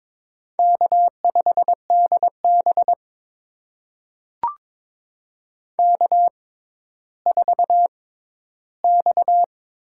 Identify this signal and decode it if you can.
Morse code, heard as a keyed tone, about 22 wpm, spaced as for 13 wpm, 700 Hz. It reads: K5DB K4X
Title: USA Call Signs - 4 Characters - 6x Character Spacing - 22wpm